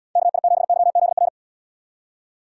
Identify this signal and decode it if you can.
Morse code, heard as a keyed tone, about 78 wpm, 700 Hz. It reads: KI7YZR